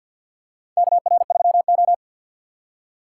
Morse code, heard as a keyed tone, about 53 wpm, 690 Hz. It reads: KR3O
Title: USA Call Signs - 4 Characters - 6x Character Spacing - 50wpm